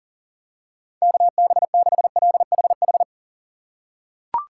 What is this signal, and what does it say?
Morse code, heard as a keyed tone, about 40 wpm, 700 Hz. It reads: KB6LHH
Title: USA Call Signs - 6 Characters - 3x Character Spacing - 40wpm